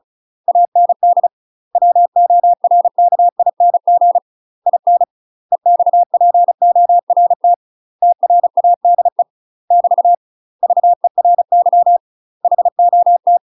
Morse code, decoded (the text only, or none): AND WORKING IN EXPORT TRADE <BT> VERY HOT